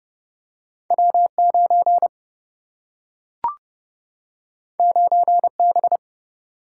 Morse code, heard as a keyed tone, about 29 wpm, 700 Hz. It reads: W9 9B
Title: Single-Letter Single-Number 30wpm